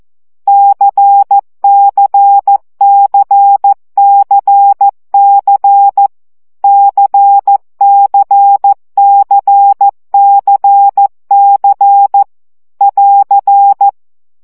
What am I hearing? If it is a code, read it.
Morse code, heard as a keyed tone, about 14 wpm, 800 Hz. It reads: CCCCC CCCCC <AR>
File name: CW gyakorlas c 12 baud